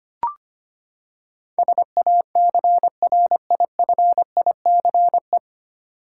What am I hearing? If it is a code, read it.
Morse code, heard as a keyed tone, about 25 wpm, 700 Hz. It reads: SACRIFICE
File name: Nine Letter Words - 25wpm